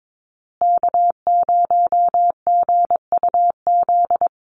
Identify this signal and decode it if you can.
Morse code, heard as a keyed tone, about 22 wpm, 700 Hz. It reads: K0GUZ
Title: Call Signs Repeated 2x - 22wpm